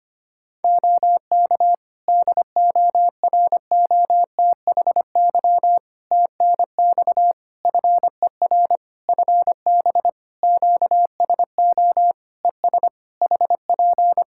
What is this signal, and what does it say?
Morse code, heard as a keyed tone, about 25 wpm, 700 Hz. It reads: OK DOROTHY TNX FER FB QSO ES HP
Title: Practice QSO - Exchange 4 - Ending - 25wpm